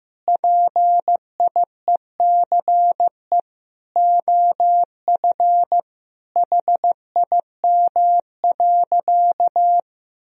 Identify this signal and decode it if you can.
Morse code, heard as a keyed tone, about 15 wpm, 700 Hz. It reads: PIECE OF HIM.
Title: Sentences from Top 1000 Words - Review Part 2 - 15wpm